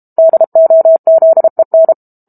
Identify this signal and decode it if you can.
Morse code, heard as a keyed tone, about 32 wpm, 650 Hz. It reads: DOZEN
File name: Book Two / 06-Chapter 6 - 32 WPM